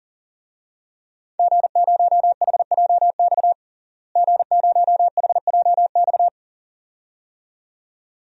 Morse code, heard as a keyed tone, about 40 wpm, 700 Hz. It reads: G0HJX G0HJX
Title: DXCC-100 Call Signs - 5 Characters - Repeated 2x - 40wpm